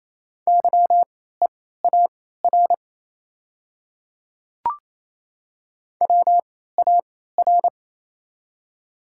Morse code, heard as a keyed tone, about 28 wpm, spaced as for 16 wpm, 700 Hz. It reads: YEAR WAR
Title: 3x Character Spacing - Top 500 Words - 28wpm